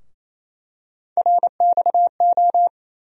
Morse code, heard as a keyed tone, about 28 wpm, 700 Hz. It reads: RXO
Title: USA Call Sign Postfixes - 3 Letters 28wpm